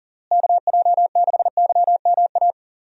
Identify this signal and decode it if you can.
Morse code, heard as a keyed tone, about 40 wpm, 700 Hz. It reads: KJ6YMA
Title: USA Call Signs - 6 Characters - Repeated 3x - 40wpm